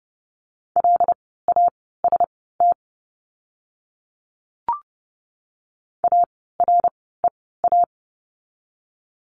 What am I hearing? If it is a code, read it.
Morse code, heard as a keyed tone, about 30 wpm, spaced as for 17 wpm, 700 Hz. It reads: LAST AREA
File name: Lesson 17f - Letter u - Words 30wpm